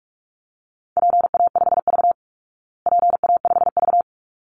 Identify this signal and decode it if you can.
Morse code, heard as a keyed tone, about 44 wpm, 700 Hz. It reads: PA5V PA5V
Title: Call Signs Repeated 3x - 45wpm